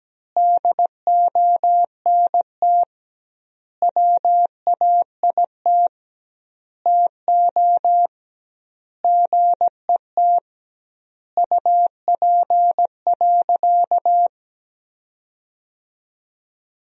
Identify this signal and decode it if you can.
Morse code, heard as a keyed tone, about 17 wpm, 700 Hz. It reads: DONT WAIT TO GET UP.
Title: 2x Word-Spacing - Sentences from Top 500 Words - 17wpm